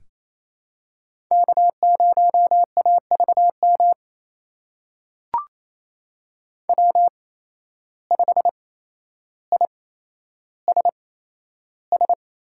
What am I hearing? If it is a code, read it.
Morse code, heard as a keyed tone, about 28 wpm, spaced as for 12 wpm, 700 Hz. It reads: K0AVM W5ISS
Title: USA Call Signs - 5 Characters - 8x Character Spacing - 28wpm